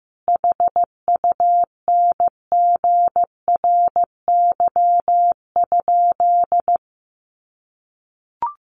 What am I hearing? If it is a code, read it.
Morse code, heard as a keyed tone, about 14 wpm, 700 Hz. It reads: HUNGRY?
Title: Sentences from Top 1400 Words 15wpm